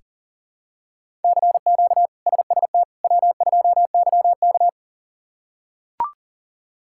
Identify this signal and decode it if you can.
Morse code, heard as a keyed tone, about 37 wpm, 700 Hz. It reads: CQ SST W2YK
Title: K1USN SST QSO Practice - 40wpm